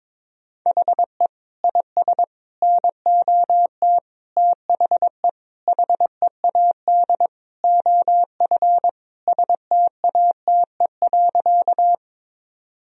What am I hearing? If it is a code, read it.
Morse code, heard as a keyed tone, about 22 wpm, 700 Hz. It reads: HE IS NOT THE HEAD OF STATE.
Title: Sentences from Top 1000 Words - Review Part 2 - 22wpm